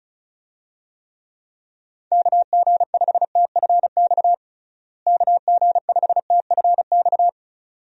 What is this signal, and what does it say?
Morse code, heard as a keyed tone, about 36 wpm, 700 Hz. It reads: KG5TFX KG5TFX
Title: USA Call Signs - 6 Characters - Repeated 2x - 35wpm